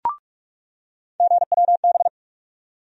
Morse code, heard as a keyed tone, about 47 wpm, 700 Hz. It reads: GWB
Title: USA Call Sign Postfixes - 3 Letters 45wpm